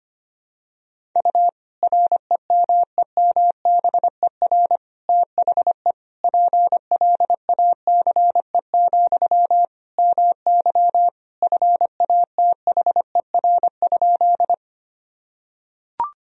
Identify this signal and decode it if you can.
Morse code, heard as a keyed tone, about 25 wpm, 700 Hz. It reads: U REMEMBER THE PLACE, MY FATHER?
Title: Sentences from Top 400 Words and Abbreviations 25wpm